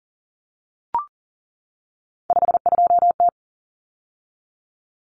Morse code, heard as a keyed tone, about 40 wpm, 700 Hz. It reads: 52T
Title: Common Cut Numbers - 3-Digit 40wpm